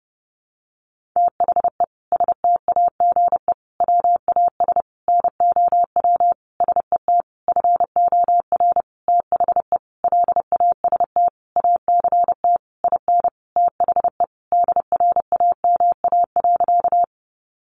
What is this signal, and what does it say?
Morse code, heard as a keyed tone, about 30 wpm, 700 Hz. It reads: THE STAGE WAS NOW SET FOR THE LAST ACT IN THE DRAMA.
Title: Sentences from Top 1500 Words 30wpm